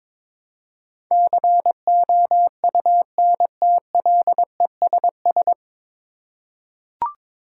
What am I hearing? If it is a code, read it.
Morse code, heard as a keyed tone, about 22 wpm, 700 Hz. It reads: COUNTLESS
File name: 9 Letter Words - First 3 Letters Given - 22wpm